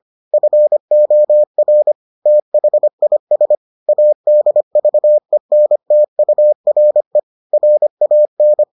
Morse code, heard as a keyed tone, about 25 wpm, 600 Hz. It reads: FOR THIS ADVENTURE RAN